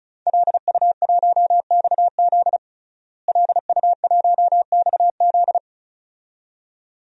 Morse code, heard as a keyed tone, about 36 wpm, 700 Hz. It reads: LU1XZ LU1XZ
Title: DXCC-100 Call Signs - 5 Characters - Repeated 2x - 35wpm